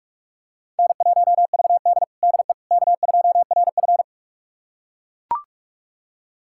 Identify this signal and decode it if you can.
Morse code, heard as a keyed tone, about 44 wpm, 690 Hz. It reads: N1VD DE K2RF
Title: USA Callsign DE Callsign - 4 Characters - 45wpm